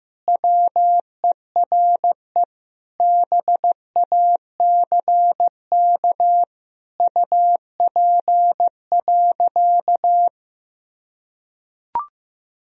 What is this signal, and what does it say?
Morse code, heard as a keyed tone, about 15 wpm, 700 Hz. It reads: WERE BACK UP.
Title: Sentences from Top 1000 Words - Review Part 2 - 15wpm